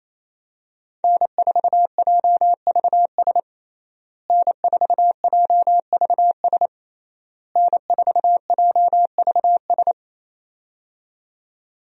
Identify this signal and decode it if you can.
Morse code, heard as a keyed tone, about 28 wpm, 700 Hz. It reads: N4JVS N4JVS N4JVS
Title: USA Call Signs - 5 Characters - Repeated 3x - 28wpm